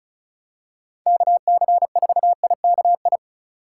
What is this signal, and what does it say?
Morse code, heard as a keyed tone, about 36 wpm, 700 Hz. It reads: KC4IKI
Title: USA Call Signs - 6 Characters - 4x Character Spacing - 35wpm